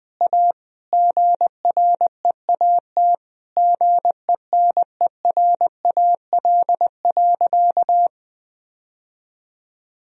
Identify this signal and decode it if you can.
Morse code, heard as a keyed tone, about 21 wpm, 700 Hz. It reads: A GREAT GENERAL.
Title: Sentences from Top 1000 Words - Review Part 3 - 20wpm